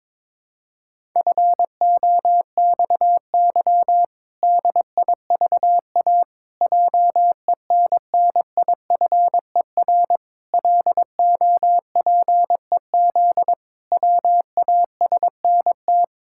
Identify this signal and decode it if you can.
Morse code, heard as a keyed tone, about 22 wpm, 700 Hz. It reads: FOXY DIVA JENNIFER LOPEZ WASNT
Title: Pangram Challenge v2 - 22wpm